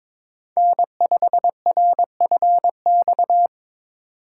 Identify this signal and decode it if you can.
Morse code, heard as a keyed tone, about 22 wpm, 700 Hz. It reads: N5RFX
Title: USA Call Signs - 5 Characters - 3x Character Spacing - 22wpm